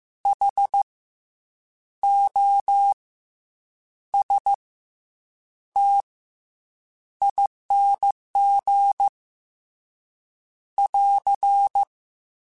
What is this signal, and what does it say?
Morse code, heard as a keyed tone, about 15 wpm, spaced as for 7 wpm, 800 Hz. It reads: HOSTING <AR>
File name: Special INGs-mp3s / Special INGs-mp3s / Fast ing @13